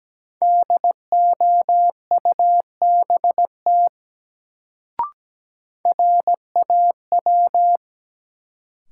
Morse code, heard as a keyed tone, about 17 wpm, 700 Hz. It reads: DOUBT RAW
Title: Sentences from Top 1500 Words - New Word Drill Alt - 17wpm